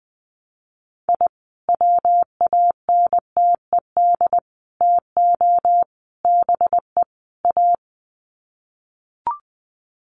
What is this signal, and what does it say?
Morse code, heard as a keyed tone, about 20 wpm, 700 Hz. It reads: I WANTED TO BE A
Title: Sets of 5 Words Encore2 20wpm